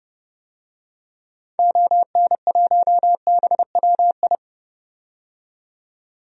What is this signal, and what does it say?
Morse code, heard as a keyed tone, about 30 wpm, 700 Hz. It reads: ON1BWI
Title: DXCC-100 Call Signs - 6 Characters - 30wpm